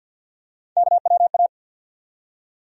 Morse code, heard as a keyed tone, about 47 wpm, 700 Hz. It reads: KWA
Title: USA Call Sign Postfixes - 3 Letters 50wpm